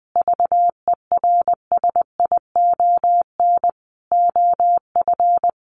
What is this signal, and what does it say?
Morse code, heard as a keyed tone, about 20 wpm, 700 Hz. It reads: VERSION OF
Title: Sentences from Top 1400 Words 20wpm